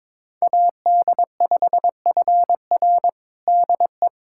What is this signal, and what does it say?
Morse code, heard as a keyed tone, about 22 wpm, 700 Hz. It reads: AD5FR DE
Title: USA Callsign DE Callsign - 5 Characters - 22wpm